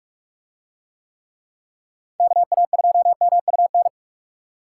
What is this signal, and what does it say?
Morse code, heard as a keyed tone, about 45 wpm, 700 Hz. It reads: KA2MUN